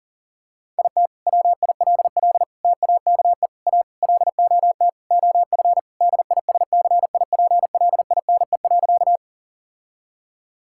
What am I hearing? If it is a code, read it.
Morse code, heard as a keyed tone, about 40 wpm, 690 Hz. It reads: IT WILL TAKE A LOT OF DISCIPLINE.
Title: Sentences from Top 1000 Words - Review Part 6 - 40wpm